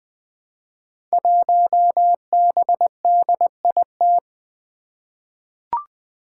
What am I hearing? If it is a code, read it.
Morse code, heard as a keyed tone, about 20 wpm, 700 Hz. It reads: 1BDIT
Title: Five Challenging Letter-Number Combinations 20wpm